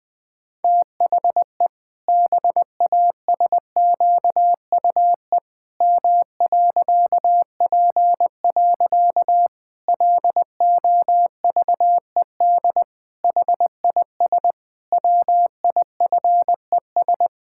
Morse code, heard as a keyed tone, about 21 wpm, 700 Hz. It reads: THE BASQUE M.P. LOVED HIS WIFES
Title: Pangram Challenge v2 - 20wpm